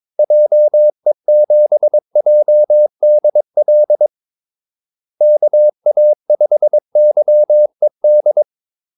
Morse code, heard as a keyed tone, about 22 wpm, 600 Hz. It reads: JE7JDL KA5YED
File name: Callsigns-22wpm